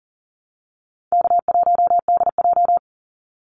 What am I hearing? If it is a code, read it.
Morse code, heard as a keyed tone, about 40 wpm, 700 Hz. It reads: K1DJ
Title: Call Signs Repeated 3x - 40wpm